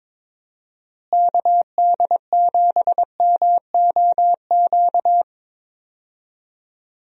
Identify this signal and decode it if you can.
Morse code, heard as a keyed tone, about 22 wpm, 700 Hz. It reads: KD7MOQ
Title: USA Call Signs - 6 Characters - Repeated 3x - 22wpm